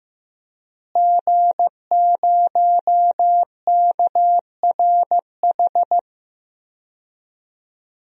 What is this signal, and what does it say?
Morse code, heard as a keyed tone, about 15 wpm, 700 Hz. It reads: G0KRH
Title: DXCC-100 Call Signs - 5 Characters - Repeated 2x - 15wpm